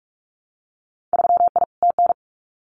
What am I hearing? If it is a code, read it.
Morse code, heard as a keyed tone, about 44 wpm, 700 Hz. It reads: 3I TN
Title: Winter Field Day Exchanges 45wpm